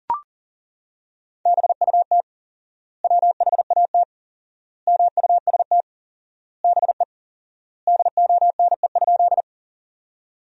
Morse code, heard as a keyed tone, about 40 wpm, 700 Hz. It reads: BUT WHAT MUST BE DONE?
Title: 4x Word-Spacing - Sentences from Top 500 Words - 40wpm